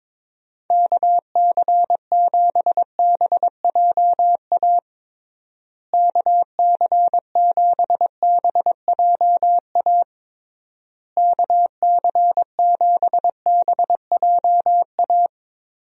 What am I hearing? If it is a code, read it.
Morse code, heard as a keyed tone, about 22 wpm, 700 Hz. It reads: KC7BJA KC7BJA KC7BJA
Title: USA Call Signs - 6 Characters - Repeated 3x - 22wpm